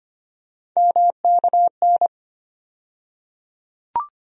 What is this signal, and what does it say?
Morse code, heard as a keyed tone, about 25 wpm, 700 Hz. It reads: MKN
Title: USA Call Sign Postfixes - 3 Letters 25wpm